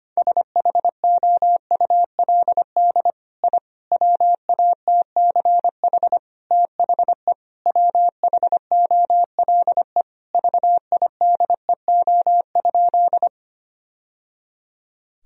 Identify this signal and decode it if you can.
Morse code, heard as a keyed tone, about 25 wpm, 700 Hz. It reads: SHOULD I WATCH THE WHOLE VIDEO?
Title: Sentences from Top 1000 Words - Review Part 4 - 25wpm